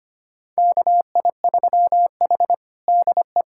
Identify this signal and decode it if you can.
Morse code, heard as a keyed tone, about 25 wpm, 700 Hz. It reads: KI3H DE
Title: USA Callsign DE Callsign - 4 Characters - 25wpm